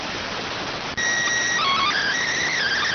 SSTV (slow-scan television), PD 290.